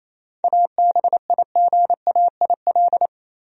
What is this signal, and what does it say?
Morse code, heard as a keyed tone, about 27 wpm, 700 Hz. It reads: ABIGAIL
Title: USA Common Names - 28wpm